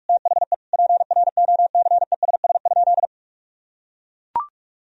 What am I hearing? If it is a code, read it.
Morse code, heard as a keyed tone, about 41 wpm, 700 Hz. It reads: THE PROCESS?
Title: Sentences from Top 1000 Words - Review Part 3 - 45wpm